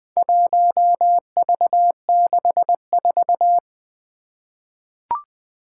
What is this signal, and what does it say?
Morse code, heard as a keyed tone, about 20 wpm, 700 Hz. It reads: 1V64
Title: Four Challenging Letter-Number Combinations 20wpm